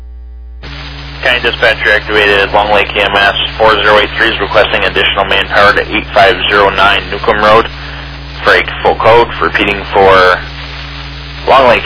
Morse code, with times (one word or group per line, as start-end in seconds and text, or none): none